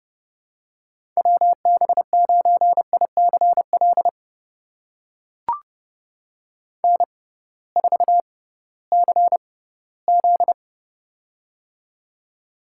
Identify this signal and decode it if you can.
Morse code, heard as a keyed tone, about 30 wpm, spaced as for 17 wpm, 700 Hz. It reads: WB9ICL N4CZ
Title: USA Call Signs - 6x Character Spacing - 30wpm